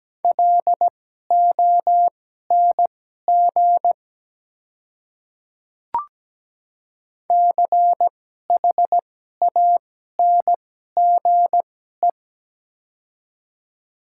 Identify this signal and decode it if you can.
Morse code, heard as a keyed tone, about 17 wpm, spaced as for 12 wpm, 700 Hz. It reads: LONG CHANGE
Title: 2x Character Spacing - Top 300 Words - 17wpm